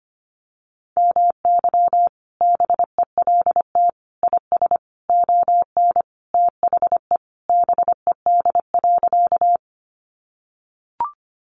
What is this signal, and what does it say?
Morse code, heard as a keyed tone, about 25 wpm, 700 Hz. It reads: MY BELT IS ON THE BED.
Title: Sentences from Top 1400 Words 25wpm